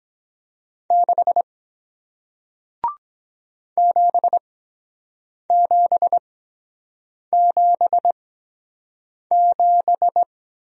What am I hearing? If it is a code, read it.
Morse code, heard as a keyed tone, about 23 wpm, spaced as for 7 wpm, 700 Hz. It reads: 6 7777